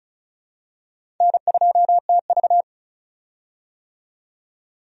Morse code, heard as a keyed tone, about 34 wpm, 700 Hz. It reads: N2TV
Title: USA Call Signs - 4 Characters - 35wpm